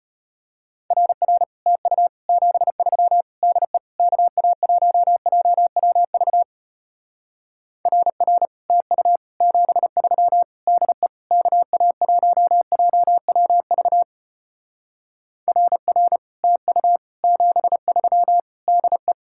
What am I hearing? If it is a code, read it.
Morse code, heard as a keyed tone, about 33 wpm, 700 Hz. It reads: RR TU 73 DE KA1JWV RR TU 73 DE KA1JWV RR TU 73 DE